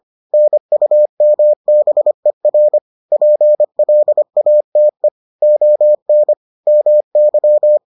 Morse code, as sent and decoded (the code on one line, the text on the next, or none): -. ..- -- -... .-. / .--. .-.. .- - . / --- -. / -- -.--
NUMBER PLATE ON MY